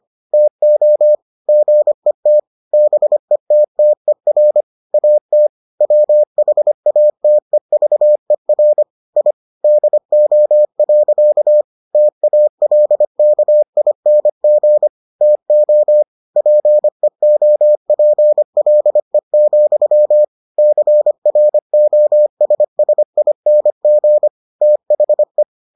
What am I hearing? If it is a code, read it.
Morse code, heard as a keyed tone, about 25 wpm, 600 Hz. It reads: TO GET BETTER AT WHATEVER I DO. TALKING TO PEOPLE, CROSSING THE